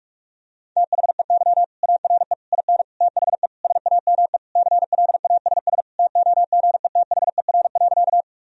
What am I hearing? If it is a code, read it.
Morse code, heard as a keyed tone, about 41 wpm, 700 Hz. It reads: THEY ARE IN THE SAME CLASS TOGETHER.
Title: Sentences from Top 1000 Words - Review Part 2 - 45wpm